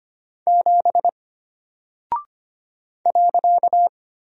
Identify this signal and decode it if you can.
Morse code, heard as a keyed tone, about 25 wpm, 700 Hz. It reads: 7 .
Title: Single Character (Letter:Number:Punctuation:Prosign) Alt 25wpm